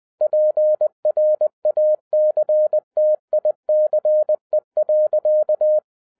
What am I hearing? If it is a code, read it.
Morse code, heard as a keyed tone, about 20 wpm, 610 Hz. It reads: PRACTICE.